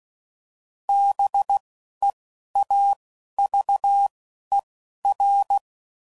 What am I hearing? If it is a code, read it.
Morse code, heard as a keyed tone, about 16 wpm, spaced as for 11 wpm, 800 Hz. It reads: BEAVER
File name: CW Code Groups @10WPM / fromSM's / Cities